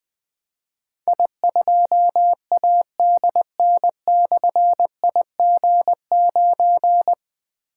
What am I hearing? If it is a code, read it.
Morse code, heard as a keyed tone, about 20 wpm, 700 Hz. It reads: I2ADN/IG9
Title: Difficult Call Signs 20wpm